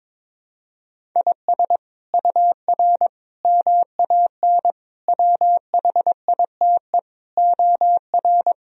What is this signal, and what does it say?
Morse code, heard as a keyed tone, about 22 wpm, 700 Hz. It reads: IS UR MAN WHITE OR
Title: Sentences from Top 400 Words and Abbreviations 22wpm